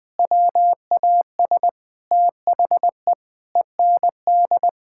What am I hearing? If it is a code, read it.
Morse code, heard as a keyed tone, about 20 wpm, 700 Hz. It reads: WAS THE END
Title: Sentences with 5-Word Phrases - Spoken Beginning - 20wpm